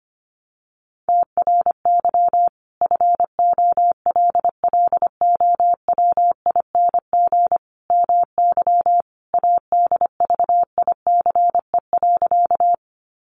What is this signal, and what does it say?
Morse code, heard as a keyed tone, about 25 wpm, 700 Hz. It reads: TRY FOLLOWING MY ADVICE.
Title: Sentences from Top 1100 Words 25wpm